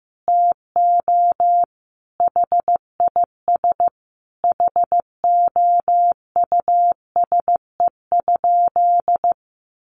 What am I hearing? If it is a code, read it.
Morse code, heard as a keyed tone, about 15 wpm, 700 Hz. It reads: TO HIS HOUSE?
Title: Sentences from Top 600 Words 15wpm